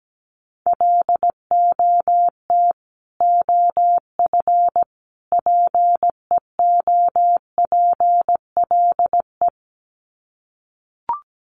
Morse code, heard as a keyed tone, about 17 wpm, 700 Hz. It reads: LOT OF PEOPLE
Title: Sets of 3 Words 17wpm